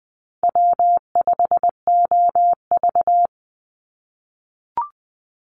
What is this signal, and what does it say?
Morse code, heard as a keyed tone, about 20 wpm, 700 Hz. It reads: W5OV
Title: Call Signs - 20wpm